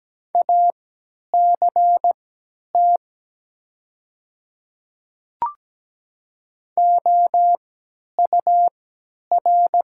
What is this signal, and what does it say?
Morse code, heard as a keyed tone, about 17 wpm, spaced as for 10 wpm, 700 Hz. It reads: ACT OUR
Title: 3x Character Spacing - Top 300 Words - 17wpm